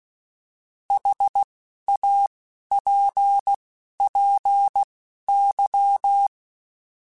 Morse code, heard as a keyed tone, about 16 wpm, spaced as for 11 wpm, 800 Hz. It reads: HAPPY
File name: Misc / Misc-mp3s / Words @10